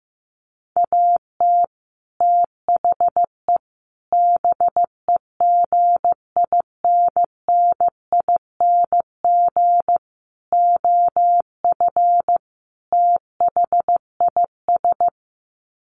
Morse code, heard as a keyed tone, about 15 wpm, 700 Hz. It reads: AT THE BEGINNING OF THIS
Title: Sets of 5 Words Encore 15wpm